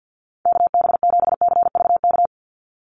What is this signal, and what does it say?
Morse code, heard as a keyed tone, about 50 wpm, 700 Hz. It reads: KB7CVK
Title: USA Call Signs 50wpm